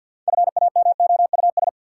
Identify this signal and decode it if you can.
Morse code, heard as a keyed tone, about 53 wpm, 700 Hz. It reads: FAMOUS
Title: Sentences from Top 1500 Words - New Word Drill - 50wpm